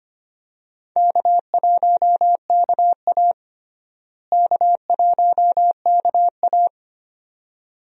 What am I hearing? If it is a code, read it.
Morse code, heard as a keyed tone, about 25 wpm, 700 Hz. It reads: K1KA K1KA
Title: USA Call Signs - 4 Characters - Repeated 2x - 25wpm